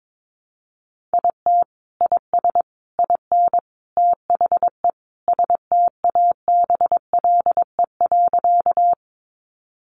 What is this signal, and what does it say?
Morse code, heard as a keyed tone, about 22 wpm, 700 Hz. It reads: IT IS IN THE STABLE.
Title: Sentences from Top 1500 Words 22wpm